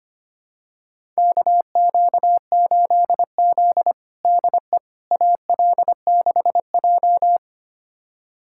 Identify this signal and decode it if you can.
Morse code, heard as a keyed tone, about 25 wpm, 700 Hz. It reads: KQ8Z DE AL6J
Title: USA Callsign DE Callsign - 4 Characters - 25wpm